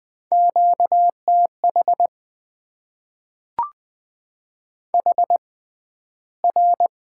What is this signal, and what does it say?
Morse code, heard as a keyed tone, about 20 wpm, spaced as for 12 wpm, 700 Hz. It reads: QTH HR
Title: Top 90 QSO Elements - 6x Character Spacing - 20wpm